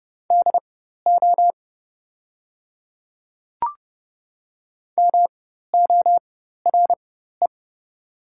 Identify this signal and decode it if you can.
Morse code, heard as a keyed tone, about 31 wpm, spaced as for 14 wpm, 700 Hz. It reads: DO MORE
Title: 4x Character Spacing - Top 100 Words - 30wpm